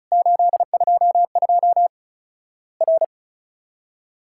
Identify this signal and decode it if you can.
Morse code, heard as a keyed tone, about 36 wpm, 700 Hz. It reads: 822 R
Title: Worked All Europe DX Contest - QTC Exchange - 35wpm